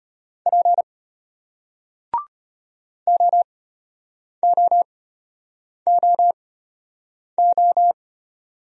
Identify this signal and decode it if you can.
Morse code, heard as a keyed tone, about 34 wpm, spaced as for 8 wpm, 690 Hz. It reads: P OOOO